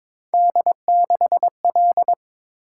Morse code, heard as a keyed tone, about 26 wpm, 700 Hz. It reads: D6L
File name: Speed-Racing - Three Letter-Number Combinations - 22wpm